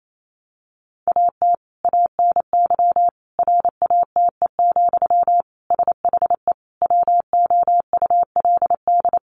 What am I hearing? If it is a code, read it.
Morse code, heard as a keyed tone, about 28 wpm, 700 Hz. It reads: AT ANY RATE, SHE WOULD
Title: Sentences from Top 1300 Words 28wpm